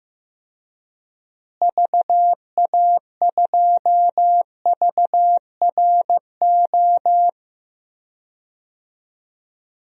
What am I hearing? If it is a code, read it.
Morse code, heard as a keyed tone, about 15 wpm, 700 Hz. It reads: VA2VRO
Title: DXCC-100 Call Signs - 6 Characters - Repeated 2x - 15wpm